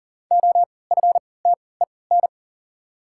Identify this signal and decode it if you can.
Morse code, heard as a keyed tone, about 40 wpm, spaced as for 23 wpm, 700 Hz. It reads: OFTEN
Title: 3x Character Spacing - Top 300 Words - 40wpm